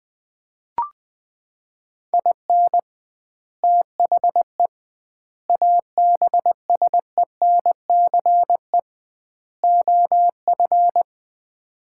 Morse code, heard as a keyed tone, about 19 wpm, 700 Hz. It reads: IN THE ABSENCE OF
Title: 2x Word-Spacing - Sets of 4 Words - 20wpm